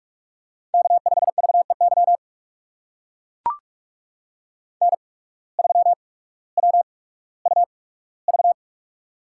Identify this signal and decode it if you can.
Morse code, heard as a keyed tone, about 44 wpm, spaced as for 19 wpm, 700 Hz. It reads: K5VEY N3WUV
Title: USA Call Signs - 5 Characters - 8x Character Spacing - 45wpm